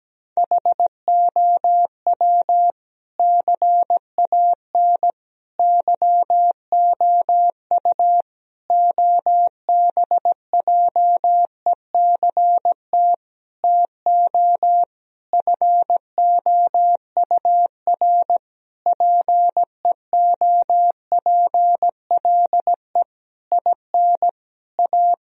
Morse code, heard as a keyed tone, about 17 wpm, 700 Hz. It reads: HOW CAN YOU OBJECT TO FOUR PEOPLE IN A
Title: Sentences from Top 1000 Words - Review Part 3 - 17wpm